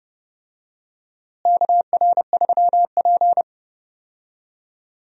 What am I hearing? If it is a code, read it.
Morse code, heard as a keyed tone, about 30 wpm, 700 Hz. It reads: KR3P
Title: USA Call Signs - 4 Characters - Repeated 2x - 30wpm